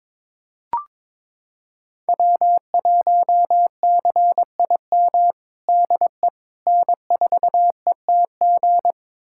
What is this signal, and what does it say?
Morse code, heard as a keyed tone, about 22 wpm, 700 Hz. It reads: W1CIM DE N4ETG
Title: USA Callsign DE Callsign - 5 Characters - 22wpm